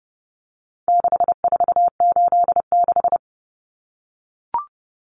Morse code, heard as a keyed tone, about 30 wpm, 700 Hz. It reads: 6486
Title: Numbers - 4-Digit 30wpm